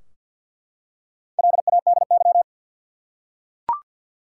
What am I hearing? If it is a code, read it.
Morse code, heard as a keyed tone, about 52 wpm, 700 Hz. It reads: LADY
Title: Four Letter Words - 50wpm